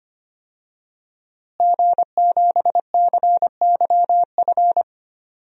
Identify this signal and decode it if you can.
Morse code, heard as a keyed tone, about 25 wpm, 700 Hz. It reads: G7CYF
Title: DXCC-100 Call Signs - 5 Characters - Repeated 2x - 25wpm